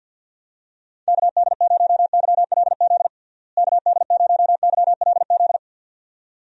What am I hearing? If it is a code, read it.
Morse code, heard as a keyed tone, about 52 wpm, 700 Hz. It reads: KD0YLZ KD0YLZ
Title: USA Call Signs - 6 Characters - Repeated 2x - 50wpm